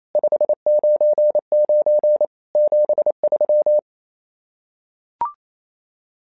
Morse code, heard as a keyed tone, about 27 wpm, 600 Hz. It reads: 599 73
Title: SOTA QSO Practice 28wpm